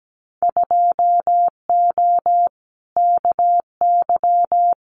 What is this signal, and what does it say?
Morse code, heard as a keyed tone, about 17 wpm, 700 Hz. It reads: 2O KY
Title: Winter Field Day Exchanges 17wpm